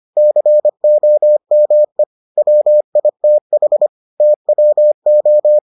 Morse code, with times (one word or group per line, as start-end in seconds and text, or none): 0.2-2.0 COME
2.4-3.9 WITH
4.2-5.6 TWO